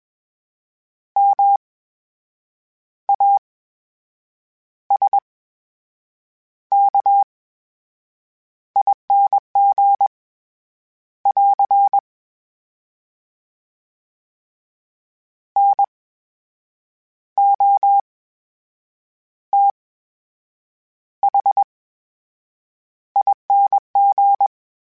Morse code, heard as a keyed tone, about 21 wpm, spaced as for 7 wpm, 800 Hz. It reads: MASKING <AR> NOTHING
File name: Special INGs-mp3s / Special INGs-mp3s / VerySlowToFaster Ing @20